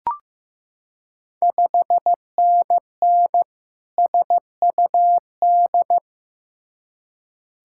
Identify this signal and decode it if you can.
Morse code, heard as a keyed tone, about 15 wpm, 690 Hz. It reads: 5NN SUD
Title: Ontario QSO Party Exchange 15wpm